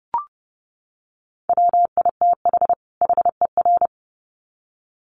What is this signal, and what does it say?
Morse code, heard as a keyed tone, about 30 wpm, 700 Hz. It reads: WITH HER
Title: Sets of 2 Words 30wpm